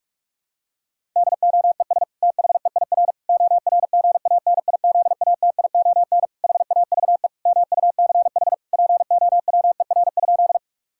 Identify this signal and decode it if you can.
Morse code, heard as a keyed tone, about 41 wpm, 690 Hz. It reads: DOES THEIR ORGANIZATION HAVE MUCH POWER?
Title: Sentences from Top 1000 Words - Review Part 5 - 45wpm